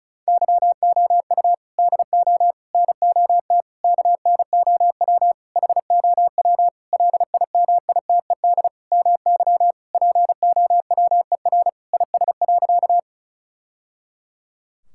Morse code, heard as a keyed tone, about 36 wpm, 700 Hz. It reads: YOU DO NOT KNOW HOW LIMITED MY POWER IS.
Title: Sentences from Top 1000 Words - Review Part 6 - 35wpm